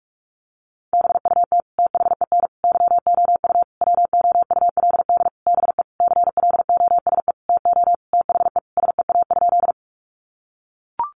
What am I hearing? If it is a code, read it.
Morse code, heard as a keyed tone, about 44 wpm, 700 Hz. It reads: BUT THEN YOU WOULD BE CLOSE TO THE SEA?
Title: Sentences from Top 300 Words 45wpm